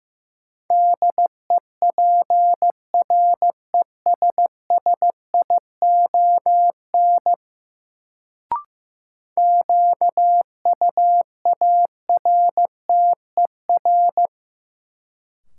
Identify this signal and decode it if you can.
Morse code, heard as a keyed tone, about 15 wpm, 700 Hz. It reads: DEPRESSION QUARTER
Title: Sentences from Top 1300 Words - New Word Drill - 15wpm